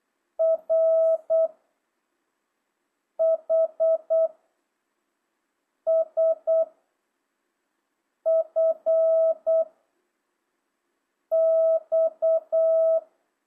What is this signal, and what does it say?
Morse code, heard as a keyed tone, about 8 wpm, 660 Hz. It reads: R H S F X
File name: jh8cbh / 2025cw